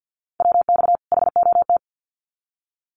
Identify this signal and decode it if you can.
Morse code, heard as a keyed tone, about 50 wpm, 700 Hz. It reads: WX HOT
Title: Common QSO Phrases 50wpm